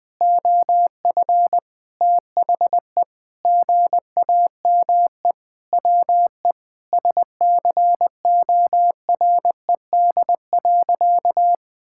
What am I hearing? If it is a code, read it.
Morse code, heard as a keyed tone, about 20 wpm, 700 Hz. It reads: OF THE GAME WE SCORED.
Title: Sentences with 6-Word Phrases - Spoken Beginning - 20wpm